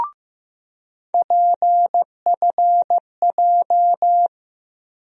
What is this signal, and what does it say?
Morse code, heard as a keyed tone, about 15 wpm, 700 Hz. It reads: PFJ
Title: USA Call Sign Postfixes - 3 Letters 15wpm